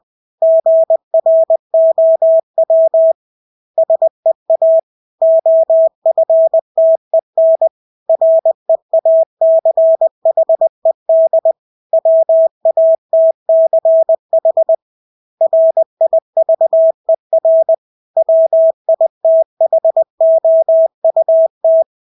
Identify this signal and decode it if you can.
Morse code, heard as a keyed tone, about 20 wpm, 650 Hz. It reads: GROW SEA OFTEN REACHED WATCH RIVER WITHOUT
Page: Engelske ord 20 wpm